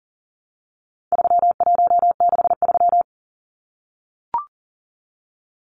Morse code, heard as a keyed tone, about 40 wpm, 700 Hz. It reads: 3163
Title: Numbers - 4-Digit 40wpm